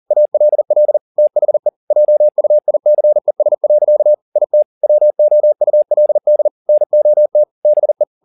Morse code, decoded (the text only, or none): ALL THE JUICES. IT WOULD NOT BE